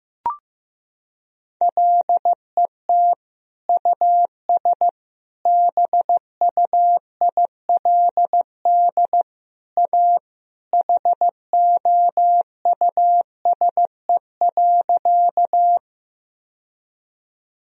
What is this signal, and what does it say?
Morse code, heard as a keyed tone, about 15 wpm, 700 Hz. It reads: LET US BUILD A HOUSE.
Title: Sentences from Top 1000 Words - Review Part 1 - 15wpm